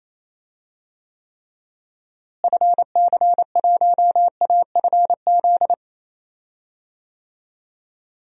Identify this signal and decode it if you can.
Morse code, heard as a keyed tone, about 28 wpm, 700 Hz. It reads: FC1AFZ
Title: DXCC-100 Call Signs - 6 Characters - Repeated 2x - 28wpm